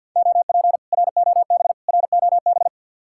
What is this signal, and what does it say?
Morse code, heard as a keyed tone, about 63 wpm, 700 Hz. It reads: OP ROB ROB